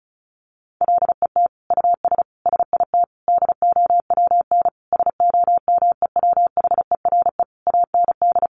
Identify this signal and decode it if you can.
Morse code, heard as a keyed tone, about 34 wpm, 700 Hz. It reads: LET US SIT DOWN SOMEWHERE AND
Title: Sentences from Top 1400 Words 35wpm